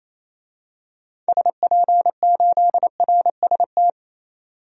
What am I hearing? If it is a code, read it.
Morse code, heard as a keyed tone, about 28 wpm, 700 Hz. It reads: SP8RST
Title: DXCC-100 Call Signs - 6 Characters - 28wpm